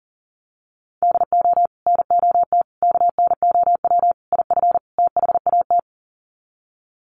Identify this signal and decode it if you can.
Morse code, heard as a keyed tone, about 40 wpm, 700 Hz. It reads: DO NOT KNOW IF THAT
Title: Sets of 5 Words Encore2 40wpm